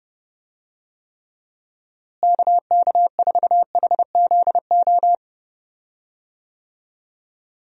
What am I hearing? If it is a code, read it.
Morse code, heard as a keyed tone, about 30 wpm, 700 Hz. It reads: KK4HZO
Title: USA Call Signs - 6 Characters - Repeated 2x - 30wpm